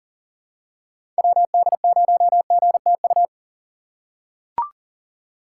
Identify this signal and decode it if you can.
Morse code, heard as a keyed tone, about 40 wpm, 700 Hz. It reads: WD0GTU
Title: USA Call Signs - 6 Characters - 4x Character Spacing - 40wpm